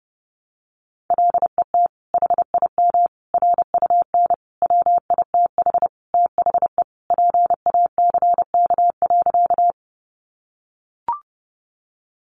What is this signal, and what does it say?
Morse code, heard as a keyed tone, about 30 wpm, 700 Hz. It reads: LET HIM RUN WITH THE PACK.
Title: Sentences from Top 1300 Words 30wpm